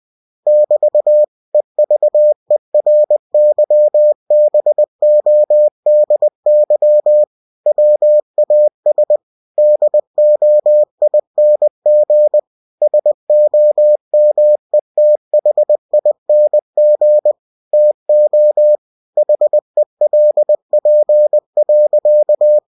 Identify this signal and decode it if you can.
Morse code, heard as a keyed tone, about 20 wpm, 600 Hz. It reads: <BT> EVERYBODY WAS DOING SOMETHING TO HELP.